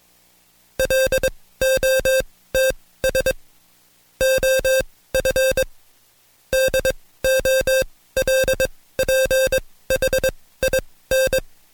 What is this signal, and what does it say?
Morse code, heard as a keyed tone, about 21 wpm, spaced as for 16 wpm, 530 Hz. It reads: LOTS OF DOLPHIN
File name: Code Quick / AudioDownld2 / Seg134-Stepper7 15WPM